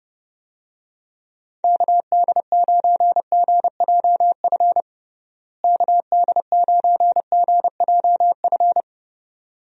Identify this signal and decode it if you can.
Morse code, heard as a keyed tone, about 30 wpm, 700 Hz. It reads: KD9GJF KD9GJF
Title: USA Call Signs - 6 Characters - Repeated 2x - 30wpm